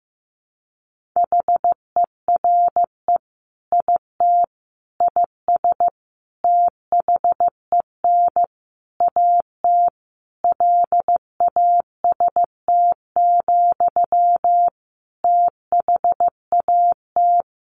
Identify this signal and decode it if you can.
Morse code, heard as a keyed tone, about 15 wpm, 700 Hz. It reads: HERE IT IS THEN AT LAST, THAT
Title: Sentences from Top 1500 Words 15wpm